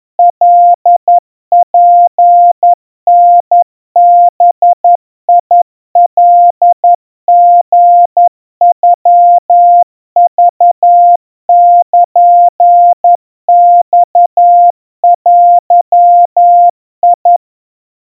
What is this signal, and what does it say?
Morse code (Japanese Wabun code), heard as a keyed tone, about 11 wpm, 700 Hz. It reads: カツタバカリノクルマデ
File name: kawa / Morse sample / kamyu 40 cpm 700 Hz Crispy